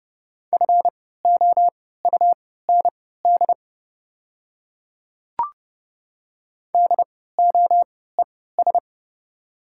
Morse code, heard as a keyed tone, about 30 wpm, spaced as for 17 wpm, 700 Hz. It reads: FOUND DOES
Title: 3x Character Spacing - Top 300 Words - 30wpm